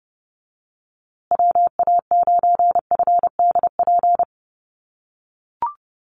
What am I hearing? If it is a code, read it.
Morse code, heard as a keyed tone, about 30 wpm, 700 Hz. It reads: WA9FDP
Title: USA Call Signs Repeated 2x 30wpm